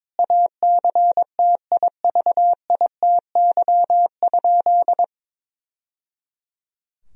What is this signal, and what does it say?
Morse code, heard as a keyed tone, about 22 wpm, 700 Hz. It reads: ACTIVITY?
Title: Sentences from Top 1000 Words - Review Part 4 - 22wpm